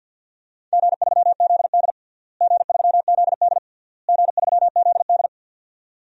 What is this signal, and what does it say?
Morse code, heard as a keyed tone, about 51 wpm, 700 Hz. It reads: G3ZD G3ZD G3ZD
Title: DXCC-100 Call Signs - 4 Characters - Repeated 3x - 50wpm